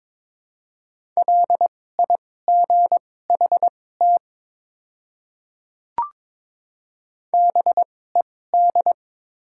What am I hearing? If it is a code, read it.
Morse code, heard as a keyed tone, about 21 wpm, spaced as for 16 wpm, 700 Hz. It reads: LIGHT BED
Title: 2x Character Spacing - Top 500 Words - 22wpm